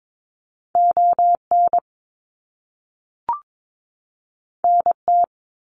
Morse code, heard as a keyed tone, about 22 wpm, 700 Hz. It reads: ON NT
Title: Canadian Provinces and Territories 22wpm